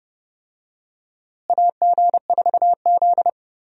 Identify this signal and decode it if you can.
Morse code, heard as a keyed tone, about 30 wpm, 700 Hz. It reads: AG4Z
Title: USA Call Signs - 4 Characters - Repeated 2x - 30wpm